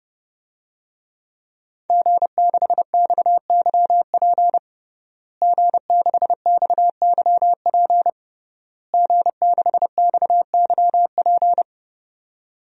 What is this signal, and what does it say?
Morse code, heard as a keyed tone, about 30 wpm, 700 Hz. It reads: G6XYP G6XYP G6XYP
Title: DXCC-100 Call Signs - 5 Characters - Repeated 3x - 30wpm